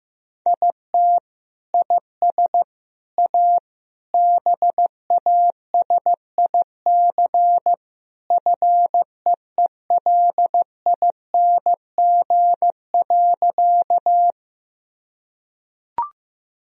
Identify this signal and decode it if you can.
Morse code, heard as a keyed tone, about 15 wpm, 700 Hz. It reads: IT IS A BASIC FEELING.
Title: Sentences from Top 1000 Words - Review Part 3 - 15wpm